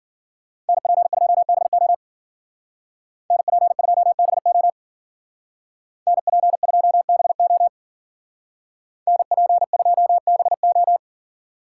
Morse code, heard as a keyed tone, about 52 wpm, 700 Hz. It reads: NP2BO NP2BO NP2BO NP2BO